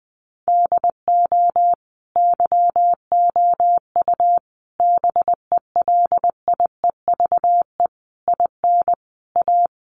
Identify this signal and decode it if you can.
Morse code, heard as a keyed tone, about 20 wpm, 700 Hz. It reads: DO YOU BELIEVE IN A
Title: Sentences from Top 1000 Words 20wpm